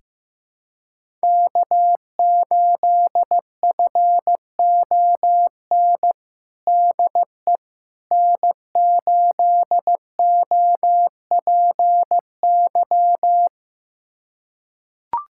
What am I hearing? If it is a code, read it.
Morse code, heard as a keyed tone, about 15 wpm, 700 Hz. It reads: K8FON DE N8OPY
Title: USA Callsign DE Callsign - 5 Characters - 15wpm